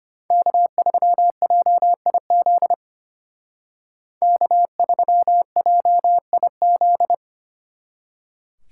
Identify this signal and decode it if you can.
Morse code, heard as a keyed tone, about 33 wpm, 700 Hz. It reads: K3JIZ K3JIZ